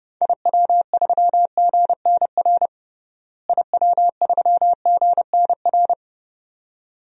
Morse code, heard as a keyed tone, about 30 wpm, 700 Hz. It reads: IW3GNR IW3GNR
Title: DXCC-100 Call Signs - 6 Characters - Repeated 2x - 30wpm